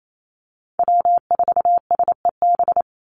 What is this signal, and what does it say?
Morse code, heard as a keyed tone, about 28 wpm, 700 Hz. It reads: W4SEB
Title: USA Call Signs 28wpm